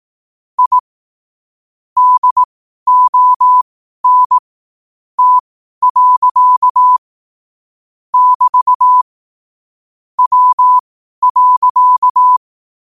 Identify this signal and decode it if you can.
Morse code, heard as a keyed tone, about 18 wpm, spaced as for 12 wpm, 1000 Hz. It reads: I DON T. <BT> W.
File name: qotdfiles / 12wpm